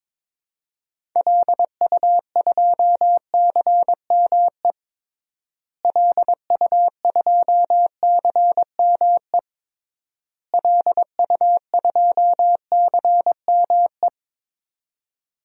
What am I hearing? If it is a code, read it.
Morse code, heard as a keyed tone, about 22 wpm, 700 Hz. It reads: LU2CME LU2CME LU2CME
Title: DXCC-100 Call Signs - 6 Characters - Repeated 3x - 22wpm